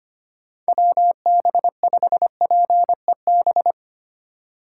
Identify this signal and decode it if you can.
Morse code, heard as a keyed tone, about 25 wpm, 700 Hz. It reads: WB5PEB
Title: USA Call Signs - 6 Characters - 2x Character Spacing - 25wpm